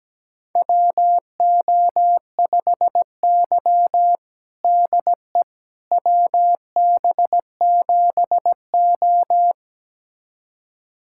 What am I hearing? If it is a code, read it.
Morse code, heard as a keyed tone, about 17 wpm, 700 Hz. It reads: WO5Y DE WB7O
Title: USA Callsign DE Callsign - 4 Characters - 17wpm